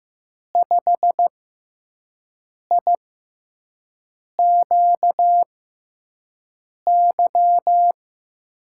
Morse code, heard as a keyed tone, about 16 wpm, spaced as for 5 wpm, 700 Hz. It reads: 5IQY
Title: USA Call Signs - 6 Characters - 6x Character Spacing - 15wpm